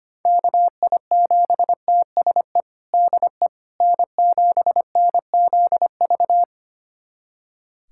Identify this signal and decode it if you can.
Morse code, heard as a keyed tone, about 25 wpm, 700 Hz. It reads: KI7TSE DE N7NZV
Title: USA Callsign DE Callsign 25wpm